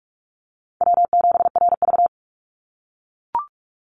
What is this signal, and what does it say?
Morse code, heard as a keyed tone, about 44 wpm, 690 Hz. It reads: W7RV
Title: NA SOTA Chaser Callsigns 45wpm